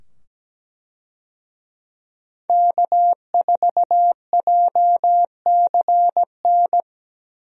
Morse code, heard as a keyed tone, about 17 wpm, 700 Hz. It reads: K4JCN
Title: USA Call Signs - 5 Characters - 17wpm